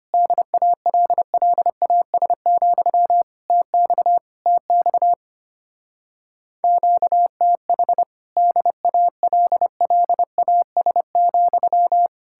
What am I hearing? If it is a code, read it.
Morse code, heard as a keyed tone, about 32 wpm, 700 Hz. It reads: DALLAS, TX TX QTH DALLAS,